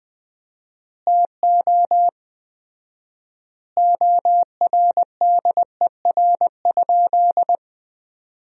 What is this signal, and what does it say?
Morse code, heard as a keyed tone, about 20 wpm, 700 Hz. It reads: TO ORDER?
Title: 4x Word-Spacing - Sentences from Top 400 Words - 20wpm